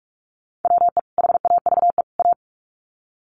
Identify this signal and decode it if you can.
Morse code, heard as a keyed tone, about 44 wpm, 700 Hz. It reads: WE HAVE A
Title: Sets of 3 Words 45wpm